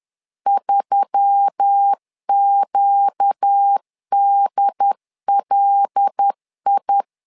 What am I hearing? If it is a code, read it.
Morse code, heard as a keyed tone, about 10 wpm, 800 Hz. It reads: 3QDLI
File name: LEZIONI / 5 LBVC / 5 LBV 10wpm 2248